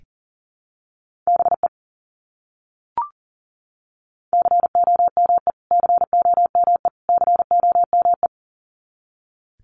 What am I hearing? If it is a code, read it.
Morse code, heard as a keyed tone, about 38 wpm, 700 Hz. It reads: BE COME COME COME
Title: Top 200 Words Repeated 40wpm